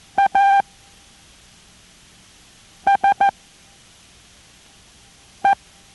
Morse code, heard as a keyed tone, about 14 wpm, spaced as for 4 wpm, 790 Hz. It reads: ASE